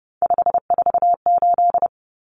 Morse code, heard as a keyed tone, about 30 wpm, 700 Hz. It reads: 548
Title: Numbers - 3-Digit 30wpm